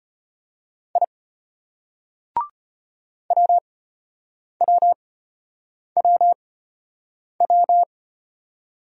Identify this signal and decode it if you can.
Morse code, heard as a keyed tone, about 34 wpm, spaced as for 8 wpm, 700 Hz. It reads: I WWWW